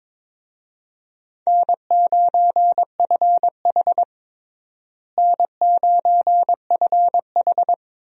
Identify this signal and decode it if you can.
Morse code, heard as a keyed tone, about 22 wpm, 700 Hz. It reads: N9FH N9FH
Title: USA Call Signs - 4 Characters - Repeated 3x - 22wpm